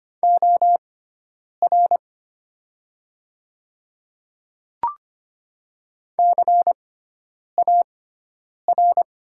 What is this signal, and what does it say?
Morse code, heard as a keyed tone, about 25 wpm, spaced as for 9 wpm, 700 Hz. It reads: OR CAR
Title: 6x Character Spacing - Top 300 Words - 25wpm